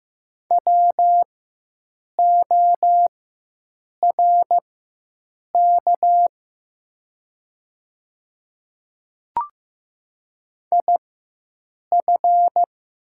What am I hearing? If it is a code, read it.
Morse code, heard as a keyed tone, about 15 wpm, spaced as for 7 wpm, 700 Hz. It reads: WORK IF
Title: 4x Character Spacing - Top 200 Words - 15wpm